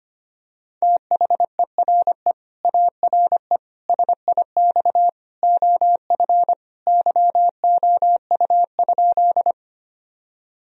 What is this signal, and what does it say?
Morse code, heard as a keyed tone, about 25 wpm, 700 Hz. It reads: THERE ARE SIX OF YOU?
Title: Sentences from Top 1000 Words - Review Part 2 - 25wpm